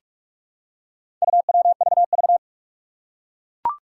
Morse code, heard as a keyed tone, about 44 wpm, 700 Hz. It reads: UWVV
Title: Four Challenging Letter-Number Combinations 45wpm